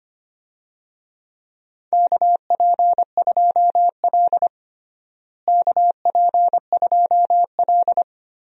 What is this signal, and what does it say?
Morse code, heard as a keyed tone, about 25 wpm, 700 Hz. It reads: KP2L KP2L
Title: DXCC-100 Call Signs - 4 Characters - Repeated 2x - 25wpm